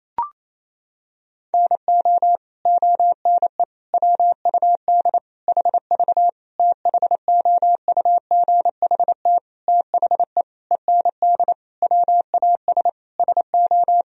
Morse code, heard as a keyed tone, about 28 wpm, 700 Hz. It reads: NO ONE WUD HV THOUGHT THE END WAS SO
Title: Sentences from Top 300 Words and Abbreviations 28wpm